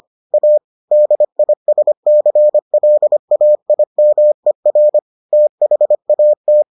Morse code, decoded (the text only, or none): A DISCLAIMER THAT